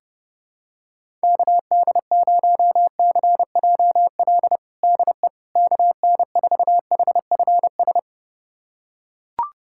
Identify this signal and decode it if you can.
Morse code, heard as a keyed tone, about 30 wpm, 700 Hz. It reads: KD0CJL DE KN4HFS
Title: USA Callsign DE Callsign 30wpm